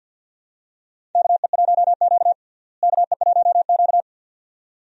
Morse code, heard as a keyed tone, about 51 wpm, 700 Hz. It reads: KE1Q KE1Q
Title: USA Call Signs - 4 Characters - Repeated 2x - 50wpm